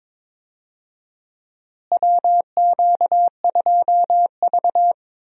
Morse code, heard as a keyed tone, about 22 wpm, 700 Hz. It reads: WQ2V
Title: USA Call Signs - 4 Characters - Repeated 2x - 22wpm